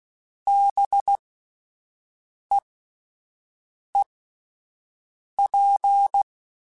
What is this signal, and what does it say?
Morse code, heard as a keyed tone, about 16 wpm, spaced as for 5 wpm, 800 Hz. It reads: BEEP